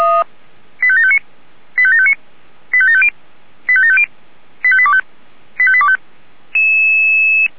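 Five-tone selective call (ZVEI), heard as quick runs of five tones, tones of about 70 ms. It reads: 85759 85759 85750 85750 85825 85825
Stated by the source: Five-tone ZVEI selective call received by SDR.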